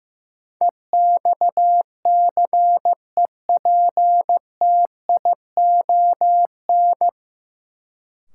Morse code, heard as a keyed tone, about 15 wpm, 700 Hz. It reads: EXCEPTION
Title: Sets of 4 Words - New Word Drill - 15wpm